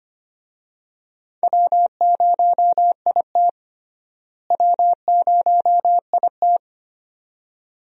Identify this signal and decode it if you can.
Morse code, heard as a keyed tone, about 25 wpm, 700 Hz. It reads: W0IT W0IT
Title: USA Call Signs - 4 Characters - Repeated 2x - 25wpm